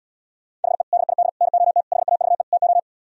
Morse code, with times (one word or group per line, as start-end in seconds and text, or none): 0.6-0.8 HE
0.9-1.3 DID
1.4-1.8 NOT
1.9-2.4 HAVE
2.5-2.8 TO